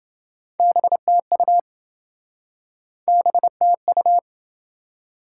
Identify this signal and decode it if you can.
Morse code, heard as a keyed tone, about 27 wpm, 700 Hz. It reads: BTU BTU